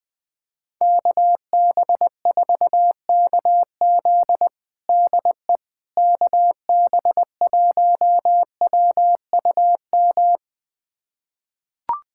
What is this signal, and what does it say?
Morse code, heard as a keyed tone, about 20 wpm, 700 Hz. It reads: KB4KZ DE KB1WUM
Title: USA Callsign DE Callsign 20wpm